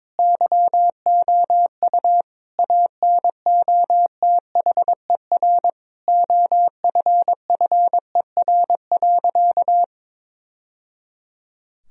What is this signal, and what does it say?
Morse code, heard as a keyed tone, about 22 wpm, 700 Hz. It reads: YOU ANOTHER OFFER.
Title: Sentences from Top 1000 Words - Review Part 4 - 22wpm